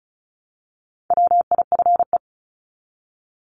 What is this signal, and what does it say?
Morse code, heard as a keyed tone, about 34 wpm, 700 Hz. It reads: WIFE
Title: Top 1751-2000 Words 35wpm